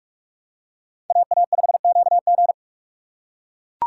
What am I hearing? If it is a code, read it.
Morse code, heard as a keyed tone, about 41 wpm, 700 Hz. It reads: AA5QG